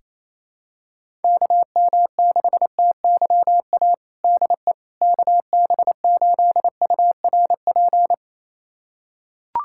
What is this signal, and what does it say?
Morse code, heard as a keyed tone, about 28 wpm, 700 Hz. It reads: KM6TYA DE KB8URP
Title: USA Callsign DE Callsign 28wpm